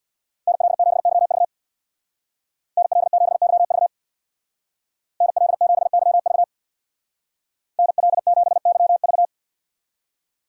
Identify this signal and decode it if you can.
Morse code, heard as a keyed tone, about 58 wpm, 700 Hz. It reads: NL7YV NL7YV NL7YV NL7YV
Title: Speed Racing - USA 2x2 Call Signs 50wpm